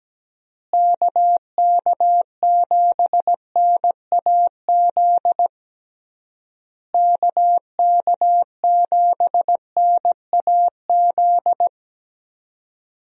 Morse code, heard as a keyed tone, about 17 wpm, 700 Hz. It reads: KK7NAZ KK7NAZ
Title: USA Call Signs - 6 Characters - Repeated 3x - 17wpm